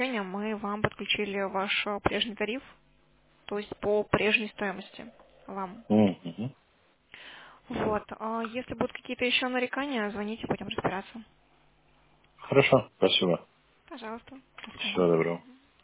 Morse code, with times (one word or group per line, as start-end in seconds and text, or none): none